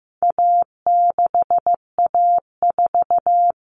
Morse code, heard as a keyed tone, about 15 wpm, 700 Hz. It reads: A6A4
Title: Common Cut Numbers - 4-Digit 15wpm